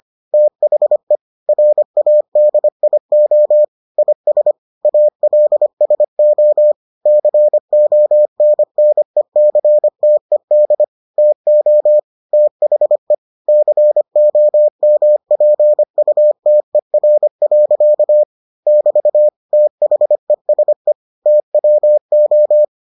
Morse code, heard as a keyed tone, about 25 wpm, 600 Hz. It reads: THE RADIO IS ALSO CONNECTED TO THE COMPUTER. <BT> THESE TWO